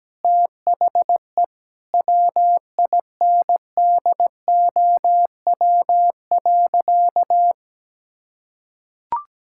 Morse code, heard as a keyed tone, about 17 wpm, 700 Hz. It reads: THE WINDOW.
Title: Sentences from Top 1600 Words 17wpm